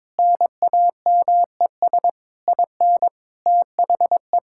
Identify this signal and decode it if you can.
Morse code, heard as a keyed tone, about 22 wpm, 700 Hz. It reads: NAMES IN THE
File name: Sentences with 7-Word Phrases - 22wpm